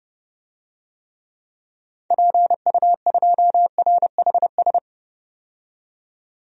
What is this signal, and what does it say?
Morse code, heard as a keyed tone, about 32 wpm, 700 Hz. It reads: PU2RHS
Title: DXCC-100 Call Signs - 6 Characters - 30wpm